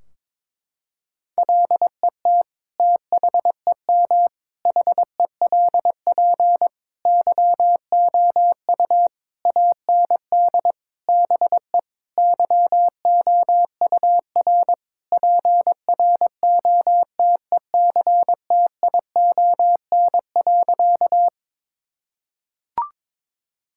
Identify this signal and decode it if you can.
Morse code, heard as a keyed tone, about 22 wpm, 700 Hz. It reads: LET THEM HELP YOU AND BE YOUR PROTECTION.
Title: Sentences from Top 1600 Words 22wpm